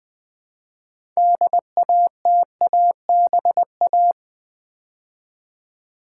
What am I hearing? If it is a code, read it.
Morse code, heard as a keyed tone, about 20 wpm, 700 Hz. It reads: DATABA
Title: Build A Word - 3 Letters At A Time - 8-Letter Words - 20wpm